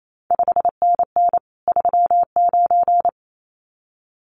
Morse code, heard as a keyed tone, about 28 wpm, 700 Hz. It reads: 5NN 39
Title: CQ WW DX Contest Exchange - 28wpm